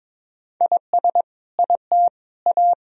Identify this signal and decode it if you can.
Morse code, heard as a keyed tone, about 21 wpm, 700 Hz. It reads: IS IT A
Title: Sentences from Top 1000 Words - Review Part 4 - 22wpm